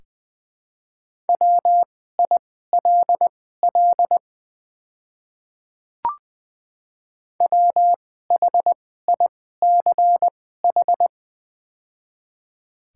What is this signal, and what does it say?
Morse code, heard as a keyed tone, about 20 wpm, spaced as for 14 wpm, 700 Hz. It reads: WILL WHICH
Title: 2x Character Spacing - Top 100 Words - 20wpm